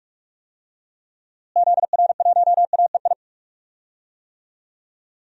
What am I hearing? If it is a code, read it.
Morse code, heard as a keyed tone, about 47 wpm, 700 Hz. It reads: ZR1AEI
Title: DXCC-100 Call Signs - 6 Characters - Repeated 3x - 45wpm